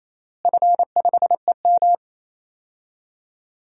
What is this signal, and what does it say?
Morse code, heard as a keyed tone, about 28 wpm, 700 Hz. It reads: F5EM
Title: DXCC-100 Call Signs - 4 Characters - Repeated 3x - 28wpm